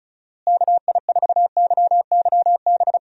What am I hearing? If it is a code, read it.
Morse code, heard as a keyed tone, about 36 wpm, 700 Hz. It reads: KI4YYB
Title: USA Call Signs - 6 Characters - Repeated 3x - 35wpm